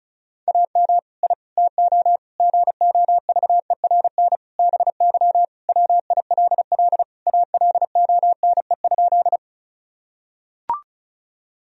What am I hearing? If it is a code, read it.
Morse code, heard as a keyed tone, about 34 wpm, 700 Hz. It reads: AM I TO GOVERN BY WILL ALONE?
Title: Sentences from Top 1000 Words - Review Part 2 - 35wpm